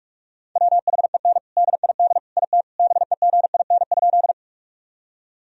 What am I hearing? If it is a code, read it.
Morse code, heard as a keyed tone, about 44 wpm, 700 Hz. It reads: WHEN DID IT BEGIN?
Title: Sentences from Top 1000 Words - Review Part 1 - 45wpm